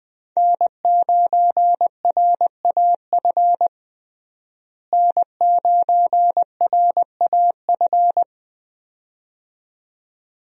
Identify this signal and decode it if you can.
Morse code, heard as a keyed tone, about 20 wpm, 700 Hz. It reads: N9RAF N9RAF
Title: USA Call Signs - 5 Characters - Repeated 2x - 20wpm